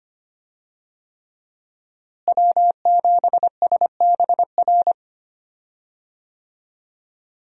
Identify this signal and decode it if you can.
Morse code, heard as a keyed tone, about 25 wpm, 700 Hz. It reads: W7SBR
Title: USA Call Signs - 5 Characters - Repeated 3x - 25wpm